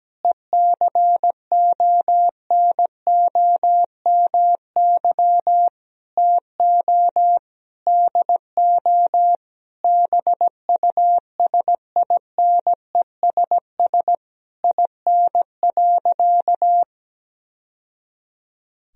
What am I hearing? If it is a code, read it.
Morse code, heard as a keyed tone, about 17 wpm, 700 Hz. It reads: ECONOMY TO DO BUSINESS IN.
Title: Sentences from Top 1000 Words - Review Part 3 - 17wpm